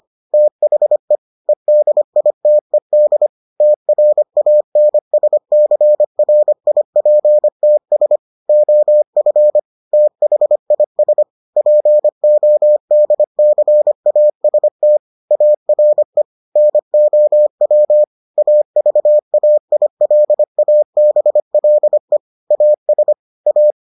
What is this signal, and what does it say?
Morse code, heard as a keyed tone, about 25 wpm, 600 Hz. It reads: THE EDITED TRANSCRIPTS OF THIS PODCAST ARE NOW AVAILABLE AS A